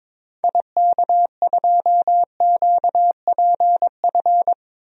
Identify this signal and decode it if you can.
Morse code, heard as a keyed tone, about 22 wpm, 700 Hz. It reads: IK2QPF
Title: DXCC-100 Call Signs - 6 Characters - Repeated 2x - 22wpm